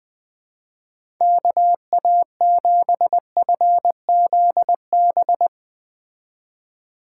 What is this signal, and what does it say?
Morse code, heard as a keyed tone, about 20 wpm, 700 Hz. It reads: KA7FZB